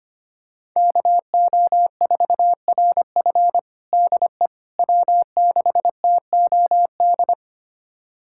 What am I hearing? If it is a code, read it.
Morse code, heard as a keyed tone, about 25 wpm, 700 Hz. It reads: KO4RF DE W6TOD
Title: USA Callsign DE Callsign - 5 Characters - 25wpm